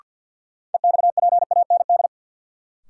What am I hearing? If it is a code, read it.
Morse code, heard as a keyed tone, about 53 wpm, 700 Hz. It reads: EXPAND
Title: Sentences from Top 1400 Words - New Word Drill Alt - 50wpm